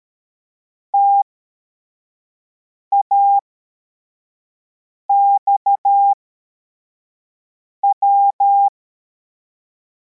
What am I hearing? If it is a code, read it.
Morse code, heard as a keyed tone, about 13 wpm, spaced as for 4 wpm, 800 Hz. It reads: TAXW
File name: formation / cw / Text74